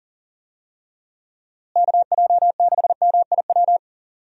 Morse code, heard as a keyed tone, about 39 wpm, 700 Hz. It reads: KJ6MIW
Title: USA Call Signs - 6 Characters - Repeated 2x - 40wpm